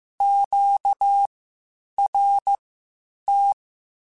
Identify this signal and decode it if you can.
Morse code, heard as a keyed tone, about 15 wpm, spaced as for 8 wpm, 800 Hz. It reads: QRT